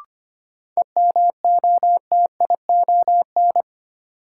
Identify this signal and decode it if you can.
Morse code, heard as a keyed tone, about 25 wpm, 700 Hz. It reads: EMOTION
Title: Sentences from Top 1600 Words - New Word Drill - 25wpm